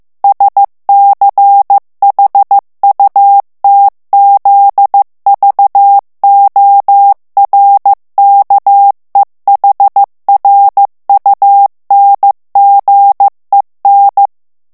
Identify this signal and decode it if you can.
Morse code, heard as a keyed tone, about 15 wpm, 800 Hz. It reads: SCHUTZVORKEHRUNGEN
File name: morse mp3 / ALLEMAND texte clair 1 vitesse 14 mots minute